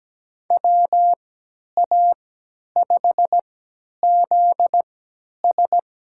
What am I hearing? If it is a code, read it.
Morse code, heard as a keyed tone, about 17 wpm, spaced as for 10 wpm, 700 Hz. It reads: WA5ZS
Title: USA Call Signs - 3x Character Spacing - 17wpm